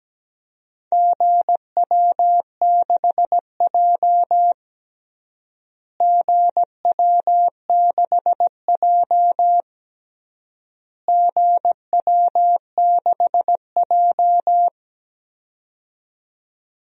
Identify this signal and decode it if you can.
Morse code, heard as a keyed tone, about 17 wpm, 700 Hz. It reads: GW6J GW6J GW6J
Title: DXCC-100 Call Signs - 4 Characters - Repeated 3x - 17wpm